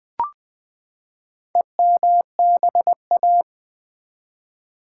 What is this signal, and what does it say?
Morse code, heard as a keyed tone, about 19 wpm, 700 Hz. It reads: EMBA
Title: Build A Word - 4 Letters At A Time - 12-Letter Words - 20wpm